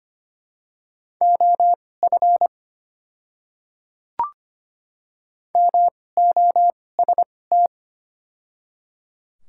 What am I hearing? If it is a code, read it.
Morse code, heard as a keyed tone, about 25 wpm, spaced as for 18 wpm, 700 Hz. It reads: OF MOST
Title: 2x Character Spacing - Top 100 Words - 25wpm